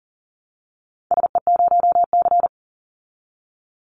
Morse code, heard as a keyed tone, about 40 wpm, 700 Hz. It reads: SE0C
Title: Call Signs Repeated 3x - 40wpm